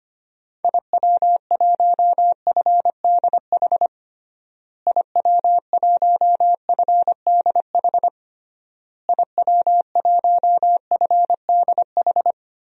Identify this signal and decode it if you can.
Morse code, heard as a keyed tone, about 25 wpm, 690 Hz. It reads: IW1FDH IW1FDH IW1FDH
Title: DXCC-100 Call Signs - 6 Characters - Repeated 3x - 25wpm